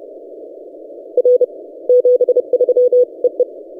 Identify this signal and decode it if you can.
Morse code, heard as a keyed tone, about 31 wpm, 500 Hz. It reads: R 73 EE
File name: posts / 4w-n1yc-17cw- -1516z